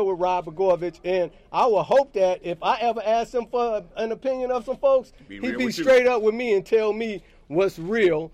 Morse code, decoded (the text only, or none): none